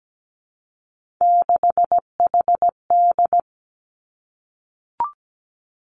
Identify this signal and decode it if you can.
Morse code, heard as a keyed tone, about 17 wpm, 700 Hz. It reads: 6HD
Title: Three Challenging Letter-Number Combinations 17wpm